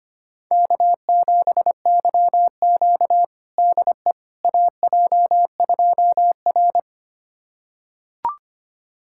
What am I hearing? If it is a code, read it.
Morse code, heard as a keyed tone, about 25 wpm, 700 Hz. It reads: K7YQ DE AJ2R
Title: USA Callsign DE Callsign - 4 Characters - 25wpm